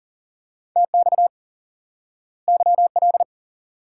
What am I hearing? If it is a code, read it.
Morse code, heard as a keyed tone, about 40 wpm, 700 Hz. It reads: TX YL
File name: IQE Course 3 - Lesson 89 - YAGI - Two Elements - 40wpm